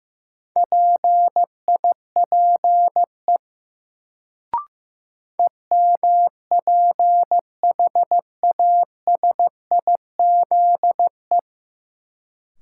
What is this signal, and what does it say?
Morse code, heard as a keyed tone, about 15 wpm, 700 Hz. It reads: PIPE EMPHASIZE
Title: Sentences from Top 1500 Words - New Word Drill - 15wpm